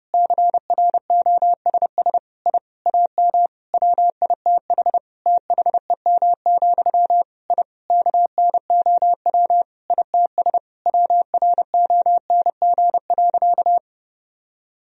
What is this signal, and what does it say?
Morse code, heard as a keyed tone, about 30 wpm, 700 Hz. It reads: CROSS I AM WITH THEM, I KNOW ITS WRONG.
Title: Sentences from Top 1000 Words - Review Part 1 - 30wpm